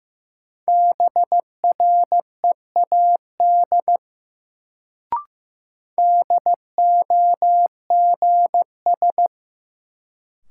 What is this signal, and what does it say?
Morse code, heard as a keyed tone, about 15 wpm, 700 Hz. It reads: BREAD DOGS
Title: Sentences from Top 1600 Words - New Word Drill - 15wpm